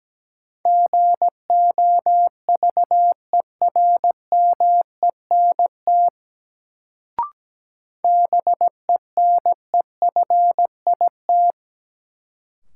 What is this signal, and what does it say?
Morse code, heard as a keyed tone, about 17 wpm, 700 Hz. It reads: GOVERMENT BENEFIT
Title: Sentences from Top 1300 Words - New Word Drill Alt - 17wpm